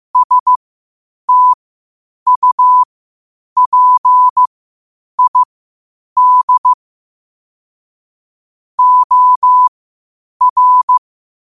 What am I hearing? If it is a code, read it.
Morse code, heard as a keyed tone, about 15 wpm, spaced as for 8 wpm, 1000 Hz. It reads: STUPID OR
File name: qotdfiles / 8wpm